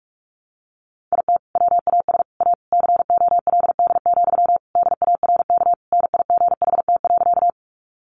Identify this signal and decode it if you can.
Morse code, heard as a keyed tone, about 44 wpm, 700 Hz. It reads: IT WAS A COLD, DARK NIGHT.
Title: Sentences from Top 500 Words 45wpm